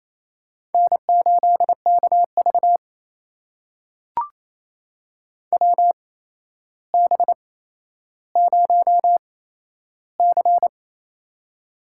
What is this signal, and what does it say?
Morse code, heard as a keyed tone, about 28 wpm, spaced as for 12 wpm, 700 Hz. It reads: N8KV WB0C
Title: USA Call Signs - 4 Characters - 8x Character Spacing - 28wpm